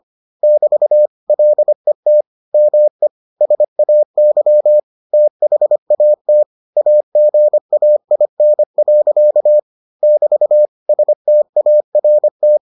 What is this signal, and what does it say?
Morse code, heard as a keyed tone, about 25 wpm, 600 Hz. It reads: <BT> LET ME SAY THAT AGAIN. <BT> START